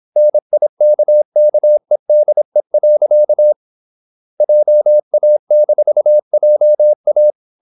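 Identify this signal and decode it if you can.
Morse code, heard as a keyed tone, about 26 wpm, 600 Hz. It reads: NIKKEDE. JA-JA
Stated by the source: High-speed (QRQ) Morse at 26 wpm.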